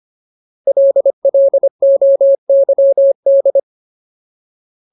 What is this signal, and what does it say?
Morse code, heard as a keyed tone, about 25 wpm, 550 Hz. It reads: LLOYD
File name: Ultimate Word List / 25WPM Ultimate